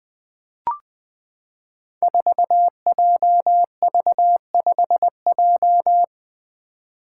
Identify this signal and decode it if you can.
Morse code, heard as a keyed tone, about 20 wpm, 700 Hz. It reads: 4JV5J
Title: Five Challenging Letter-Number Combinations 20wpm